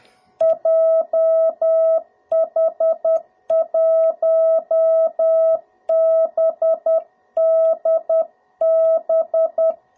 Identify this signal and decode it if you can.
Morse code, heard as a keyed tone, about 10 wpm, 660 Hz. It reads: JH1BDB
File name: jh8cbh / 2021cw / cw02b